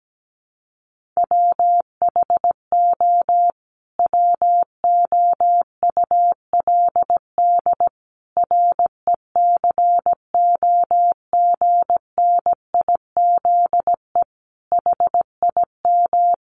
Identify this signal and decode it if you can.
Morse code, heard as a keyed tone, about 17 wpm, 700 Hz. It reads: WHO WOULD RECOGNIZE HIM
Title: Sentences from Top 1400 Words 17wpm